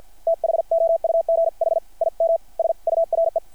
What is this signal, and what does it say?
Morse code, heard as a keyed tone, about 50 wpm, spaced as for 44 wpm, 660 Hz. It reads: THOUGH IM SURE
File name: archived / CRSnet